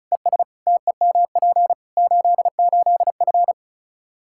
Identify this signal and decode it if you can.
Morse code, heard as a keyed tone, about 36 wpm, 700 Hz. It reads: ES TEMP 88F
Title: K1USN SST QSO Practice 35wpm